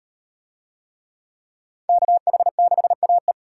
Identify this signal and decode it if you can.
Morse code, heard as a keyed tone, about 38 wpm, 700 Hz. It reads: KH6AE